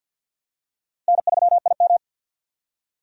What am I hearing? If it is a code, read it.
Morse code, heard as a keyed tone, about 51 wpm, 700 Hz. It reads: N3IM
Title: USA Call Signs - 4 Characters - 50wpm